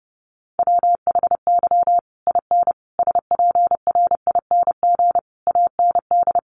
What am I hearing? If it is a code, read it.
Morse code, heard as a keyed tone, about 30 wpm, 700 Hz. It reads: WHY IN SPRING AND
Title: Sentences from Top 1200 Words 30wpm